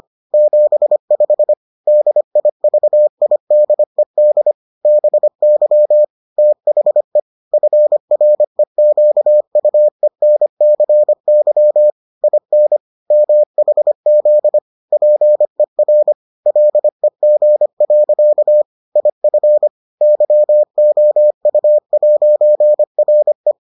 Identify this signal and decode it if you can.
Morse code, heard as a keyed tone, about 25 wpm, 600 Hz. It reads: 75 DIVIDED BY THE FREQUENCY IN MHZ PER LEG. IF YOU'RE